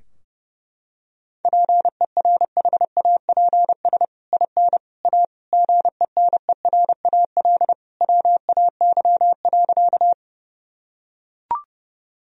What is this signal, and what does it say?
Morse code, heard as a keyed tone, about 30 wpm, 700 Hz. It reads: PERHAPS IN A GENERAL WAY.
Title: Sentences from Top 1000 Words - Review Part 3 - 30wpm